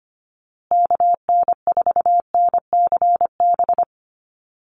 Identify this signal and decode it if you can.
Morse code, heard as a keyed tone, about 25 wpm, 700 Hz. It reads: KN4NCB
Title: USA Call Signs Repeated 3x 25wpm